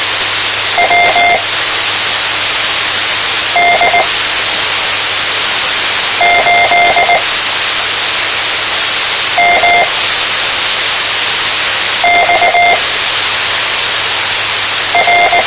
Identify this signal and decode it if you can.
Morse code, heard as a keyed tone, about 19 wpm, spaced as for 4 wpm, 700 Hz, with radio noise. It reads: WD8MXR
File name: tmranets / Elmer 04 19 2015